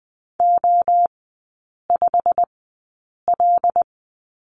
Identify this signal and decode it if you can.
Morse code, heard as a keyed tone, about 20 wpm, spaced as for 8 wpm, 700 Hz. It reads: O5L
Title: 100 Lesson 17e - Letter u - Three Characters 20wpm